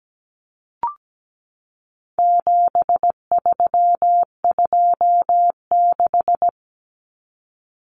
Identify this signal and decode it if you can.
Morse code, heard as a keyed tone, about 17 wpm, 700 Hz. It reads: 7326
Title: Common Cut Numbers - 4-Digit 17wpm